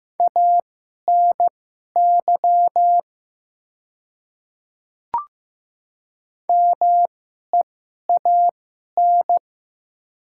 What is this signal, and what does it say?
Morse code, heard as a keyed tone, about 15 wpm, spaced as for 11 wpm, 700 Hz. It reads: ANY MEAN
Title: 2x Character Spacing - Top 300 Words - 15wpm